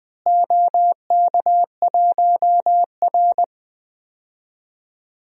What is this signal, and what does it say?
Morse code, heard as a keyed tone, about 20 wpm, 700 Hz. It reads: OK1R
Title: DXCC-100 Call Signs - 4 Characters - Repeated 3x - 20wpm